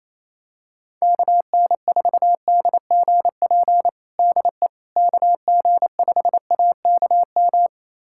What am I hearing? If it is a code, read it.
Morse code, heard as a keyed tone, about 28 wpm, 700 Hz. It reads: KN4DGP DE KG5AKM
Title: USA Callsign DE Callsign - 6 Characters - 28wpm